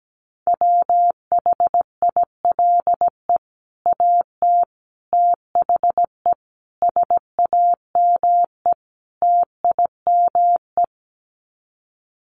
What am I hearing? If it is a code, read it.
Morse code, heard as a keyed tone, about 17 wpm, 700 Hz. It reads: WHILE AT THE SAME TIME
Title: Sets of 5 Words 17wpm